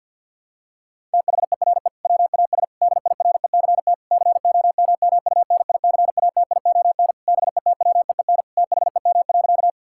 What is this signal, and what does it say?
Morse code, heard as a keyed tone, about 50 wpm, 700 Hz. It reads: THERE WAS DIRECT COMMUNICATION BETWEEN THEM.
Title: Sentences from Top 1000 Words - Review Part 6 - 50wpm